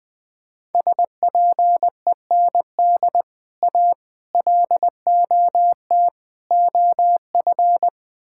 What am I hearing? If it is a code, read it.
Morse code, heard as a keyed tone, about 20 wpm, 700 Hz. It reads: SPEND A LOT OF
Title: Build A Phrase - 5-Word Phrases - 20wpm